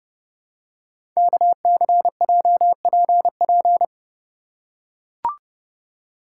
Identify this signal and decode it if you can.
Morse code, heard as a keyed tone, about 30 wpm, 700 Hz. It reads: KCJPP
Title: Code Groups - 5 Letters - 30wpm